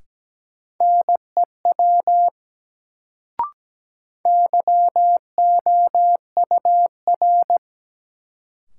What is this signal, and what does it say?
Morse code, heard as a keyed tone, about 17 wpm, 700 Hz. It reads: NEW YOUR
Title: Sets of 2 Words - New Word Drill - 17wpm